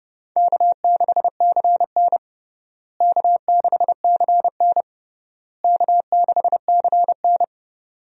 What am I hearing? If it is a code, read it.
Morse code, heard as a keyed tone, about 30 wpm, 700 Hz. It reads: K6CN K6CN K6CN
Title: USA Call Signs - 4 Characters - Repeated 3x - 30wpm